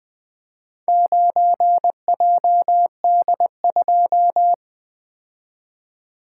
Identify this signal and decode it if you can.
Morse code, heard as a keyed tone, about 20 wpm, 700 Hz. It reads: 9JD2
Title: Four Letter-Number Combinations - 20wpm